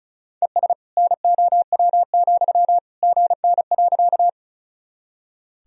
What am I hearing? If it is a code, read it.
Morse code, heard as a keyed tone, about 36 wpm, 700 Hz. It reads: ES NOW, GN.
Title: Sentences from Top 300 Words and Abbreviations 35wpm